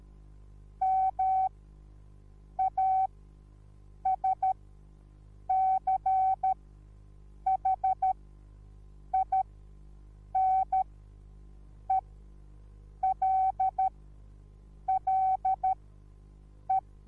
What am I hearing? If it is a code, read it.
Morse code, heard as a keyed tone, about 13 wpm, spaced as for 6 wpm, 760 Hz. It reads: MASCHINELLE